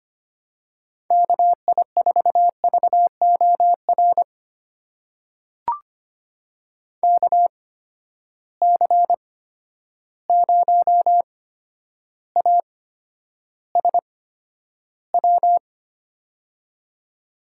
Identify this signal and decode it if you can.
Morse code, heard as a keyed tone, about 25 wpm, spaced as for 11 wpm, 700 Hz. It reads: KI4VOR KC0ASW
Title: USA Call Signs - 6 Characters - 8x Character Spacing - 25wpm